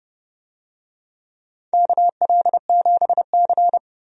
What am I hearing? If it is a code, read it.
Morse code, heard as a keyed tone, about 30 wpm, 700 Hz. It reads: KL7C
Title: DXCC-100 Call Signs - 4 Characters - 30wpm